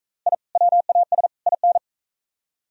Morse code, heard as a keyed tone, about 41 wpm, 700 Hz. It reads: I WAS IN